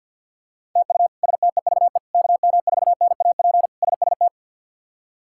Morse code, heard as a keyed tone, about 52 wpm, 700 Hz. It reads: TU STEVE KM4NAP SST
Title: K1USN SST QSO Practice - 50wpm